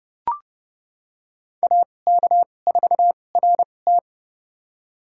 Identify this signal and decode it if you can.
Morse code, heard as a keyed tone, about 32 wpm, spaced as for 22 wpm, 700 Hz. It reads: AK4RT
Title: USA Call Signs - 5 Characters - 2x Character Spacing - 30wpm